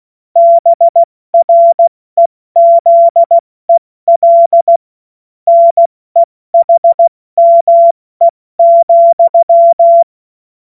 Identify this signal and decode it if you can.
Morse code, heard as a keyed tone, about 16 wpm, 680 Hz. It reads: BREZEL NEHME,
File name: Morsekurs / Witze / Witz 15. November 2022 70 F80